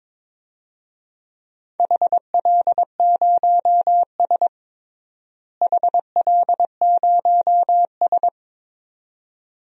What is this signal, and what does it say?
Morse code, heard as a keyed tone, about 22 wpm, 700 Hz. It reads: HL0S HL0S
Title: DXCC-100 Call Signs - 4 Characters - Repeated 2x - 22wpm